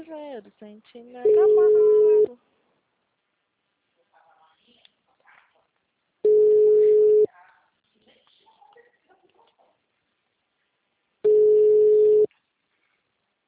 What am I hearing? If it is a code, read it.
Ringback tone (Europe, 425 Hz), with speech.